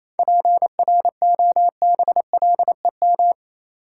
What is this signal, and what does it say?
Morse code, heard as a keyed tone, about 28 wpm, 700 Hz. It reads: PROBLEM
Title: Sentences from Top 1000 Words - New Word Drill - 28wpm